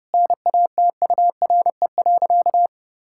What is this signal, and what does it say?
Morse code, heard as a keyed tone, about 30 wpm, 700 Hz. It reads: NATURE.
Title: Sentences from Top 1000 Words - Review Part 6 - 30wpm